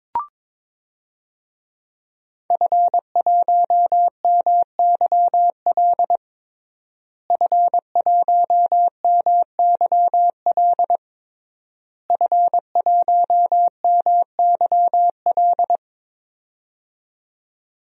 Morse code, heard as a keyed tone, about 23 wpm, 700 Hz. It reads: F1MYL F1MYL F1MYL
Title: DXCC-100 Call Signs - 5 Characters - Repeated 3x - 22wpm